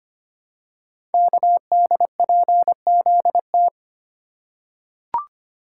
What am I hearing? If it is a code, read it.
Morse code, heard as a keyed tone, about 25 wpm, 700 Hz. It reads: KDPZT
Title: Code Groups - 5 Letters - 25wpm